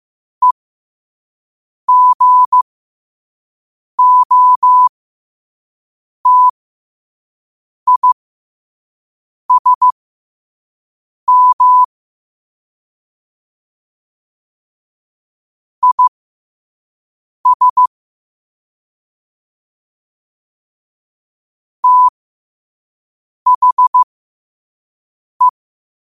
Morse code, heard as a keyed tone, about 15 wpm, spaced as for 5 wpm, 1000 Hz. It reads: EGOTISM IS THE
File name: 5wpm